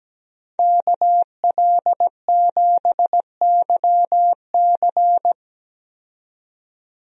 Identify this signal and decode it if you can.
Morse code, heard as a keyed tone, about 17 wpm, 700 Hz. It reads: KL7YC
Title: DXCC-100 Call Signs - 5 Characters - 17wpm